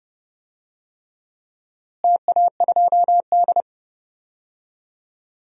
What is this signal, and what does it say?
Morse code, heard as a keyed tone, about 32 wpm, 700 Hz. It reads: TA2D
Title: DXCC-100 Call Signs - 4 Characters - Repeated 3x - 30wpm